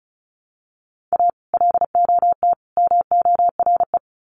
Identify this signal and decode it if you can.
Morse code, heard as a keyed tone, about 34 wpm, 700 Hz. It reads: A LOT MORE
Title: Sets of 3 Words 35wpm